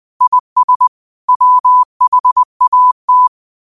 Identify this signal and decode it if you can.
Morse code, heard as a keyed tone, about 20 wpm, 1000 Hz. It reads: IS WHAT